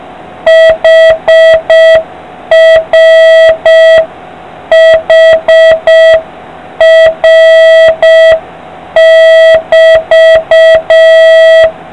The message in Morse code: HRHR<BT>

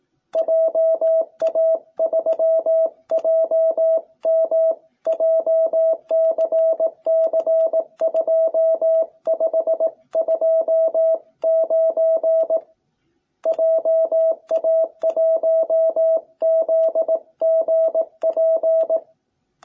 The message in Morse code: JA3JM JCC2529 JA1ZGP

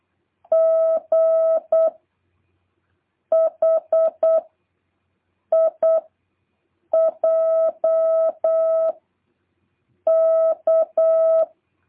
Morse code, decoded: G H I J K

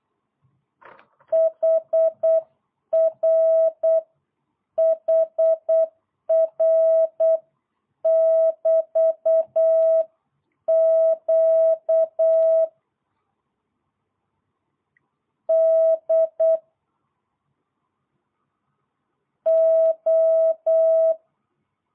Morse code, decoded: HR HR <BT> Q D O

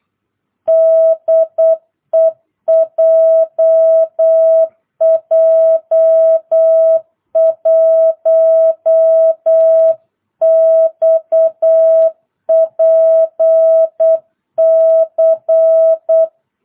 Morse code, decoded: DE JJ1XPC

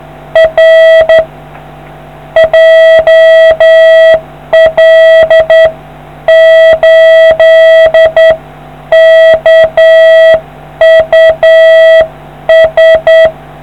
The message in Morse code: R JL8KUS

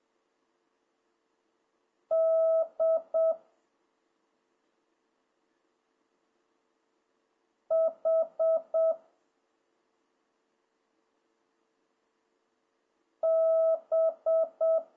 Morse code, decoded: D H B

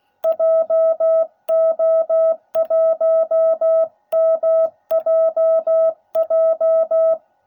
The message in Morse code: JO1MJJ

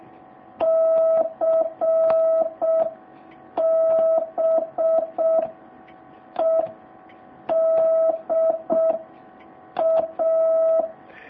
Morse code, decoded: CBEDA

